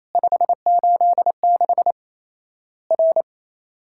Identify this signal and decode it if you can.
Morse code, heard as a keyed tone, about 28 wpm, 700 Hz. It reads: 586 R
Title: Worked All Europe DX Contest - QTC Exchange - 28wpm